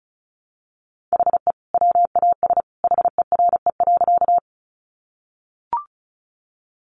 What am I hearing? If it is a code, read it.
Morse code, heard as a keyed tone, about 34 wpm, 700 Hz. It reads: HE WAS HERE.
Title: Sentences from Top 200 Words 35wpm